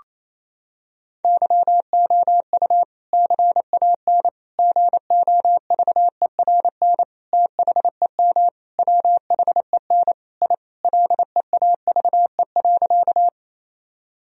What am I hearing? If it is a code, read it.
Morse code, heard as a keyed tone, about 28 wpm, 700 Hz. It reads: YOU CAN GOVERN THEM WHEN I LEAVE.
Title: Sentences from Top 1000 Words - Review Part 2 - 28wpm